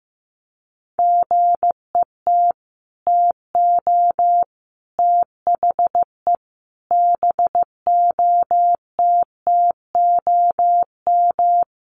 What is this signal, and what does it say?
Morse code, heard as a keyed tone, about 15 wpm, 700 Hz. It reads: GET TO THE BOTTOM
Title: Sets of 5 Words Encore 15wpm